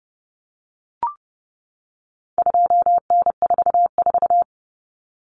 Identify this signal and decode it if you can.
Morse code, heard as a keyed tone, about 30 wpm, 700 Hz. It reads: 2N44
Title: Common Cut Numbers - 4-Digit 30wpm